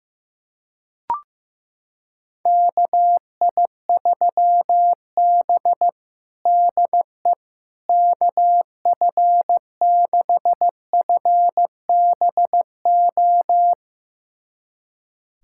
Morse code, heard as a keyed tone, about 15 wpm, 700 Hz. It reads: KI3B DE KF6FBO
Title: USA Callsign DE Callsign 15wpm